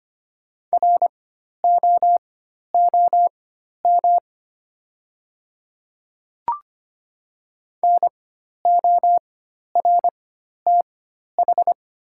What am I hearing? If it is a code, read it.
Morse code, heard as a keyed tone, about 25 wpm, spaced as for 12 wpm, 700 Hz. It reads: ROOM NORTH
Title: 4x Character Spacing - Top 300 Words - 25wpm